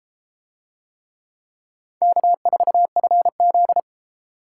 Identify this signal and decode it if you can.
Morse code, heard as a keyed tone, about 32 wpm, 700 Hz. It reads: K4FZ